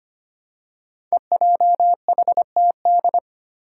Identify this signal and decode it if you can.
Morse code, heard as a keyed tone, about 25 wpm, 700 Hz. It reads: EJHTD
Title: Five Challenging Letter-Number Combinations 25wpm